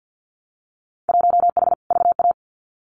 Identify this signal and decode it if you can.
Morse code, heard as a keyed tone, about 50 wpm, 700 Hz. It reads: 1H VA
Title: Winter Field Day Exchanges 50wpm